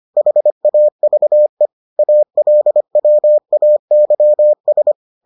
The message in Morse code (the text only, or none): HAVE ALWAYS